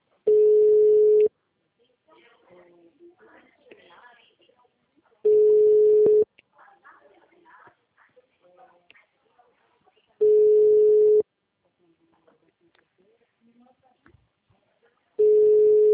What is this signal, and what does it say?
Ringback tone (Europe, 425 Hz), with speech.